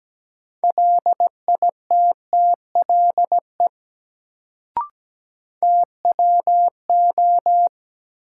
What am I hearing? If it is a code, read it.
Morse code, heard as a keyed tone, about 17 wpm, 700 Hz. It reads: LITTLE TWO
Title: Sets of 2 Words - New Word Drill - 17wpm